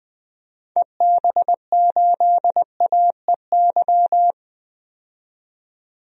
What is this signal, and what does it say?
Morse code, heard as a keyed tone, about 20 wpm, 700 Hz. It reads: EB8AEY
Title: DXCC-100 Call Signs - 6 Characters - 20wpm